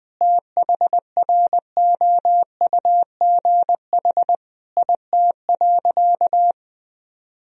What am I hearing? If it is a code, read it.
Morse code, heard as a keyed tone, about 21 wpm, 700 Hz. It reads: THROUGH IT.
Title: Sentences from Top 1000 Words - Review Part 1 - 20wpm